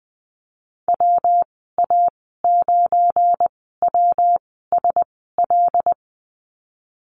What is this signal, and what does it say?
Morse code, heard as a keyed tone, about 19 wpm, spaced as for 14 wpm, 700 Hz. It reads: WA9WSL
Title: 245 Lesson 39g - Letter z - Call Signs 20wpm